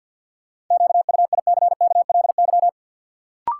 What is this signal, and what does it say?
Morse code, heard as a keyed tone, about 52 wpm, 700 Hz. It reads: QUICKLY
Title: Sentences from Top 700 Words - New Word Drill - 50wpm